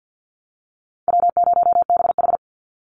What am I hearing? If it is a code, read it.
Morse code, heard as a keyed tone, about 50 wpm, 700 Hz. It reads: W0BH
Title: Easy Call Signs - 50wpm